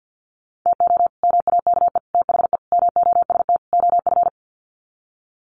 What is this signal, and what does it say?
Morse code, heard as a keyed tone, about 50 wpm, 700 Hz. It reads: TO MAKE THE MOST OF
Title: Sets of 5 Words Encore2 50wpm